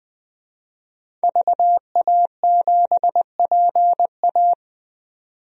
Morse code, heard as a keyed tone, about 20 wpm, 700 Hz. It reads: VA7PA
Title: DXCC-100 Call Signs - 5 Characters - 20wpm